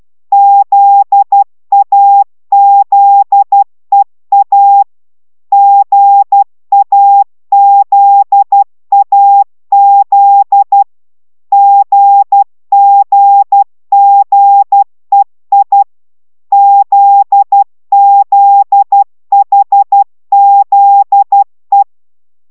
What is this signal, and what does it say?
Morse code, heard as a keyed tone, about 12 wpm, 800 Hz. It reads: ZAZEA GAZAZ GGGEI ZZHZE